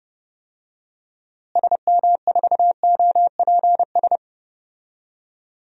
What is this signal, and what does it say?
Morse code, heard as a keyed tone, about 32 wpm, 700 Hz. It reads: SM4OPS
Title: DXCC-100 Call Signs - 6 Characters - Repeated 3x - 30wpm